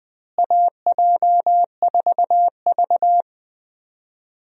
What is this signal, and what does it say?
Morse code, heard as a keyed tone, about 20 wpm, 700 Hz. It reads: AJ4V
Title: USA Call Signs - 4 Characters - Repeated 3x - 20wpm